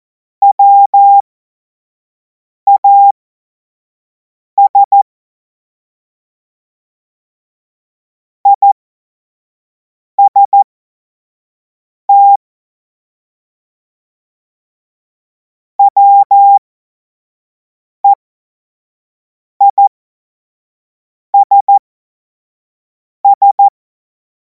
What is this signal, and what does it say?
Morse code, heard as a keyed tone, about 14 wpm, spaced as for 5 wpm, 800 Hz. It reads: WAS IST WEISS
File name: Morsekurs / Witze / Witz 14. Juli 2026 25 F70